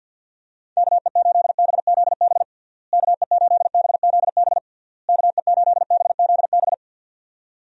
Morse code, heard as a keyed tone, about 51 wpm, 700 Hz. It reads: KE8BZB KE8BZB KE8BZB
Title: USA Call Signs - 6 Characters - Repeated 3x - 50wpm